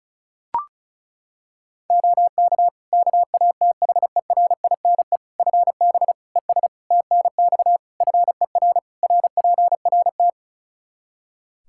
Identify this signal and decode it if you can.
Morse code, heard as a keyed tone, about 36 wpm, 700 Hz. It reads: OK KATHERINE FB ES TNX FER RPRT
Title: K1USN SST QSO Practice 35wpm